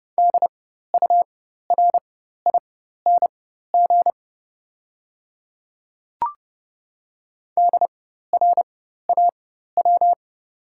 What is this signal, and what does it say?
Morse code, heard as a keyed tone, about 29 wpm, spaced as for 14 wpm, 690 Hz. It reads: DURING DRAW
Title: 4x Character Spacing - Top 500 Words - 30wpm